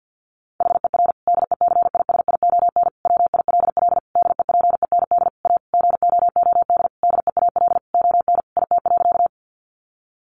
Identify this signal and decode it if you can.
Morse code, heard as a keyed tone, about 48 wpm, 700 Hz. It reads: HER DECISION WILL DEPEND A GOOD DEAL ON IT.
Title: Sentences from Top 1300 Words 50wpm